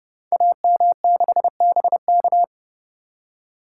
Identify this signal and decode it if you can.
Morse code, heard as a keyed tone, about 30 wpm, 700 Hz. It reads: AM6BK
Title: Code Groups - 5 Letter:Numbers - 30wpm